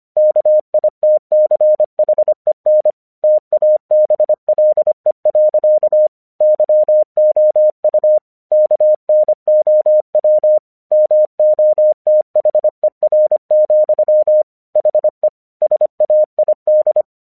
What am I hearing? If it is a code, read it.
Morse code, heard as a keyed tone, about 26 wpm, 610 Hz. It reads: KITCHEN TABLE. YOU KNOW MOTHER, HE SAID
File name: 1 English 25wpm / AQC36